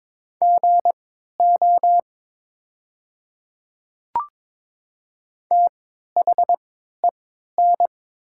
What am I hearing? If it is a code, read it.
Morse code, heard as a keyed tone, about 22 wpm, spaced as for 12 wpm, 700 Hz. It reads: GO THEN
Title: 3x Character Spacing - Top 100 Words - 22wpm